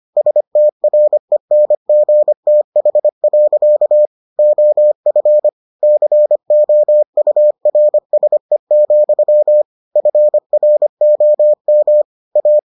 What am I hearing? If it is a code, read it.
Morse code, heard as a keyed tone, about 25 wpm, 600 Hz. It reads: STRENGTH. OF COURSE, FROM A